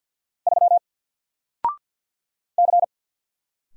Morse code, heard as a keyed tone, about 50 wpm, 700 Hz. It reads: 3 C
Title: Single Letter-Number 50wpm